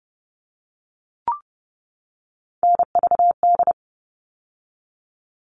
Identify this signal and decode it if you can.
Morse code, heard as a keyed tone, about 30 wpm, 700 Hz. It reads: NVD
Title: All Cut Numbers - 3-Digit 30wpm